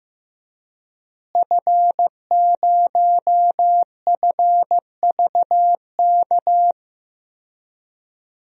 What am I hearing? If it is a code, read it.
Morse code, heard as a keyed tone, about 15 wpm, 700 Hz. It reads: F0FVK
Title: DXCC-100 Call Signs - 5 Characters - Repeated 3x - 15wpm